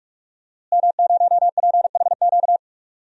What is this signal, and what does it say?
Morse code, heard as a keyed tone, about 46 wpm, 700 Hz. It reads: M0PHQ
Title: DXCC-100 Call Signs - 5 Characters - Repeated 3x - 45wpm